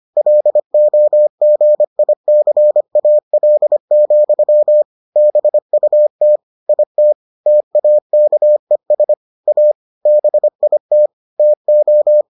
Morse code, heard as a keyed tone, about 25 wpm, 600 Hz. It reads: LOGICAL, BUT IT TAKES A BIT TO